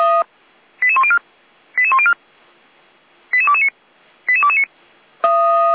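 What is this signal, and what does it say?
Five-tone selective call (ZVEI), heard as quick runs of five tones, tones of about 70 ms. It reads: 88104 88104 88229 88229
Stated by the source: ZVEI five-tone selective call received by SDR, fire brigade Feuerwehr Orlamünde.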